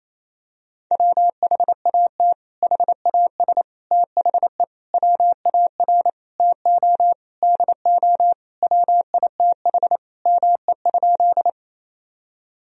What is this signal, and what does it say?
Morse code, heard as a keyed tone, about 28 wpm, 700 Hz. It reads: WHAT HAS THE WAR TO DO WITH ME?
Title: Sentences from Top 1000 Words - Review Part 2 - 28wpm